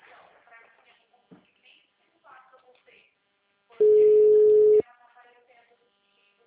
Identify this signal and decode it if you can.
Ringback tone (Europe, 425 Hz), with speech.
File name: out- -2009- - - .28673